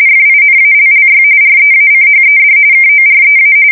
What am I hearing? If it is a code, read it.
RTTY, about 45.45 baud. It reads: FINALLY REACHED KOREA
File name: fsk170 / battleshipnj